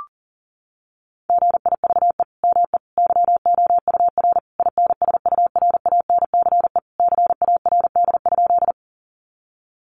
Morse code, heard as a keyed tone, about 40 wpm, 700 Hz. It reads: GIVE ME YOUR INSURANCE CARD?
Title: Sentences from Top 1100 Words 40wpm